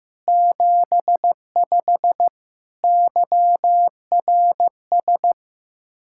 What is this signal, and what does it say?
Morse code, heard as a keyed tone, about 15 wpm, 700 Hz. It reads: 75 YRS
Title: Practice QSO - Exchange 3 - More Info - 15wpm